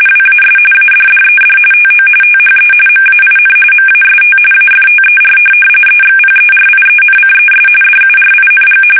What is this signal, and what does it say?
RTTY, about 45.45 baud. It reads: WAS ADDED TO THE MAIN MAST AFTER EXPERIENCES DURING
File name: fsk850 / USS-Massachusetts